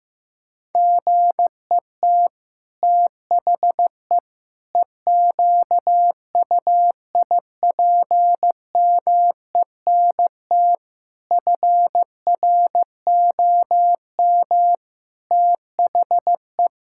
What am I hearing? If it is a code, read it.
Morse code, heard as a keyed tone, about 15 wpm, 700 Hz. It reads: GET THE EQUIPMENT FROM THE
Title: Sentences from Top 1000 Words - Review Part 5 - 15wpm